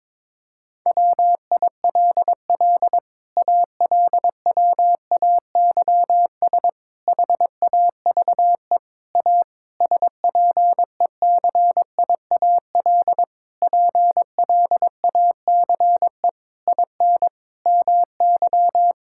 Morse code, heard as a keyed tone, about 22 wpm, 700 Hz. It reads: WILL ALWAYS HAVE A SPECIAL PLACE IN MY
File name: Sets of 9 Words - 22wpm